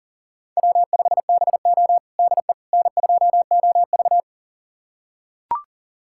Morse code, heard as a keyed tone, about 40 wpm, 700 Hz. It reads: W5BO DE N2OV
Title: USA Callsign DE Callsign - 4 Characters - 40wpm